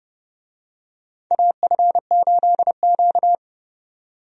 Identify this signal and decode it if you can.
Morse code, heard as a keyed tone, about 32 wpm, 700 Hz. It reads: AF8Q
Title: USA Call Signs - 4 Characters - 30wpm